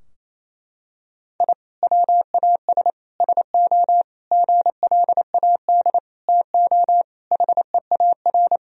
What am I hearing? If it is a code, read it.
Morse code, heard as a keyed tone, about 28 wpm, 700 Hz. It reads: I WAS SO GLAD TO HEAR
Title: Sentences from Top 1000 Words - Review Part 4 - 28wpm